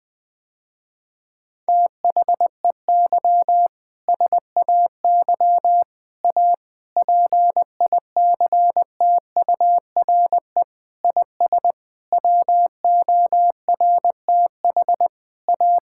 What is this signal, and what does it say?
Morse code, heard as a keyed tone, about 21 wpm, 700 Hz. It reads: THEY SAY A PICTURE IS WORTH A
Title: Sentences with 7-Word Phrases - Spoken Beginning - 20wpm